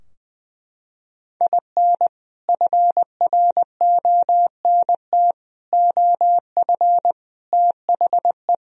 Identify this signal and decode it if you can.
Morse code, heard as a keyed tone, about 20 wpm, 700 Hz. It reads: IN FRONT OF THE
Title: Sentences from Top 1000 Words - Review Part 4 - 20wpm